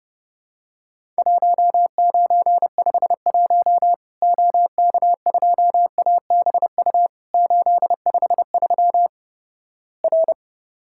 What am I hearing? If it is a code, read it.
Morse code, heard as a keyed tone, about 32 wpm, 700 Hz. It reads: 1951 OK2ABU 853 R
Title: Worked All Europe DX Contest - QTC Exchange - 30wpm